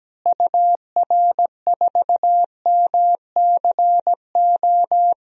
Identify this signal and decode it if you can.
Morse code, heard as a keyed tone, about 17 wpm, 700 Hz. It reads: UR4MCO
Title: DXCC-100 Call Signs - 6 Characters - Repeated 2x - 17wpm